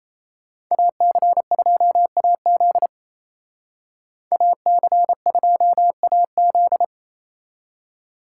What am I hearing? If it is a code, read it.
Morse code, heard as a keyed tone, about 36 wpm, 700 Hz. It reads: AC2AZ AC2AZ